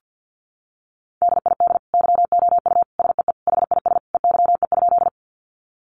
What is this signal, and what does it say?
Morse code, heard as a keyed tone, about 48 wpm, 700 Hz. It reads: DID YOU SEE HIS EYE?
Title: Sentences from Top 300 Words 50wpm